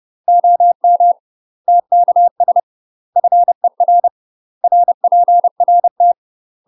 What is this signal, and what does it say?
Morse code, heard as a keyed tone, about 30 wpm, 700 Hz. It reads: OM TKS FER RPRT
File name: UK MASTER QSOs TRAINING / 30WPM / UK MASTER QSOs 0044